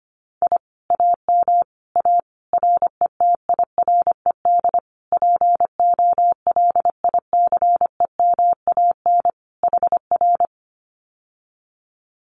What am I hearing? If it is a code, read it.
Morse code, heard as a keyed tone, about 25 wpm, 700 Hz. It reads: I AM A RETIRED POLICEMAN HR
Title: Common QSO Phrases Encore 25wpm